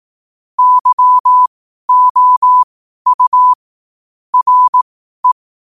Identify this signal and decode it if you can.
Morse code, heard as a keyed tone, about 18 wpm, spaced as for 12 wpm, 1000 Hz. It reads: YOU RE
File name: qotdfiles / 12wpm